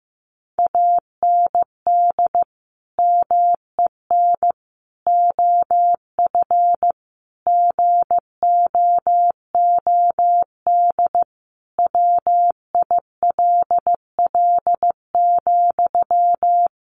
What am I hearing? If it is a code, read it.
Morse code, heard as a keyed tone, about 15 wpm, 700 Hz. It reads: AND MEN OF GOOD WILL,
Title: Sentences from Top 1400 Words 15wpm